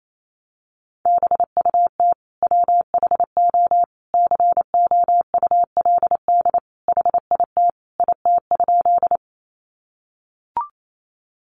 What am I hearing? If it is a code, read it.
Morse code, heard as a keyed tone, about 28 wpm, 700 Hz. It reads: BUT WHO COULD HIT IT?
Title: Sentences from Top 1200 Words 28wpm